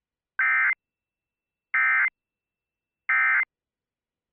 Emergency Alert System (SAME) tones, heard as harsh buzzing bursts. It is heard from 0.4 to 3.4 s.